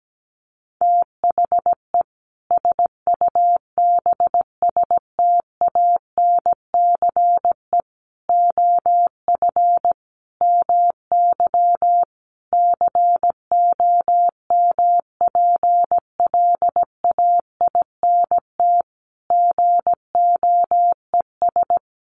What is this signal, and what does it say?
Morse code, heard as a keyed tone, about 17 wpm, 700 Hz. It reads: THE SUBSTANCE OF MY COMPLAINT GOES
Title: Sentences from Top 1500 Words 17wpm